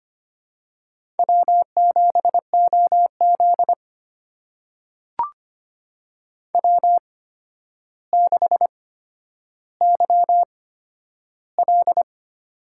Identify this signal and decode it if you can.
Morse code, heard as a keyed tone, about 25 wpm, spaced as for 11 wpm, 700 Hz. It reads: W7OZ W6YL
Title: USA Call Signs - 4 Characters - 8x Character Spacing - 25wpm